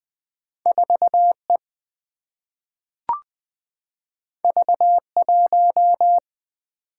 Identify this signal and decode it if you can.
Morse code, heard as a keyed tone, about 20 wpm, 700 Hz. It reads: 4E V1
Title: Single-Letter Single-Number 20wpm